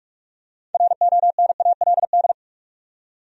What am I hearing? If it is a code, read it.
Morse code, heard as a keyed tone, about 46 wpm, 700 Hz. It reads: RONALD
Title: USA Common Names - 45wpm